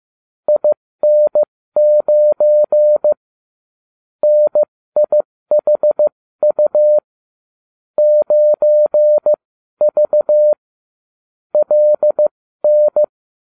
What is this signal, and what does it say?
Morse code, heard as a keyed tone, about 15 wpm, spaced as for 13 wpm, 600 Hz. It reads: IN9 NIHU 9V LN